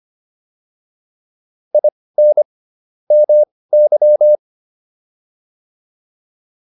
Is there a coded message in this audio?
Yes: Morse code from 1.7 to 4.4 s, heard as a keyed tone, about 25 wpm, spaced as for 18 wpm, 600 Hz. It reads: IN MY